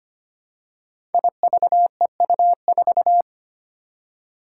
Morse code, heard as a keyed tone, about 25 wpm, 700 Hz. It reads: IVEU4
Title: Five Challenging Letter-Number Combinations 25wpm